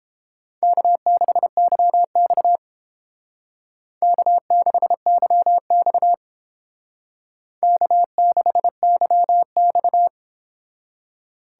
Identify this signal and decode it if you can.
Morse code, heard as a keyed tone, about 30 wpm, 700 Hz. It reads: K6YX K6YX K6YX